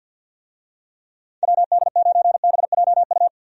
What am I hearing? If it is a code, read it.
Morse code, heard as a keyed tone, about 48 wpm, 700 Hz. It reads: WD9BJU
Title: USA Call Signs - 6 Characters - 50wpm